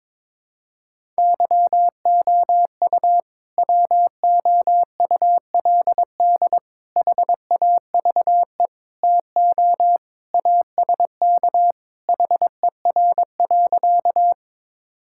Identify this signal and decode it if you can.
Morse code, heard as a keyed tone, about 22 wpm, 700 Hz. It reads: YOU WOULD HAVE TO ASK HER.
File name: practice / 22wpm / Sentences with 3-Word Phrases - 22wpm